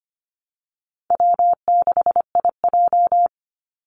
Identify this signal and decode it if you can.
Morse code, heard as a keyed tone, about 25 wpm, 700 Hz. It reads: W6IJ
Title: Easy Call Signs Repeated 2x - 25wpm